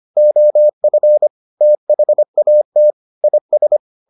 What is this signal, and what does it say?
Morse code, heard as a keyed tone, about 25 wpm, 600 Hz. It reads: OF THAT IS